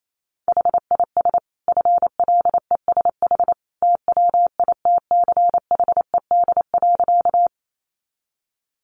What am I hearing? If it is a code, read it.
Morse code, heard as a keyed tone, about 28 wpm, 700 Hz. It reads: HIS FLESH TWITCHED.
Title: A Sound Of Thunder 28wpm